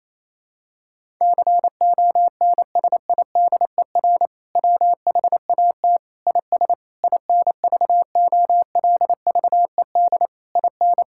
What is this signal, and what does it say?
Morse code, heard as a keyed tone, about 27 wpm, 700 Hz. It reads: CONSIDER WHAT IS INVOLVED IN
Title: Sentences from Top 1000 Words - Review Part 5 - 28wpm